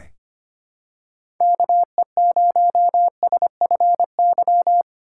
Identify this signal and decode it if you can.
Morse code, heard as a keyed tone, about 25 wpm, 700 Hz. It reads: KE0SFY
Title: USA Call Signs - 6 Characters - 2x Character Spacing - 25wpm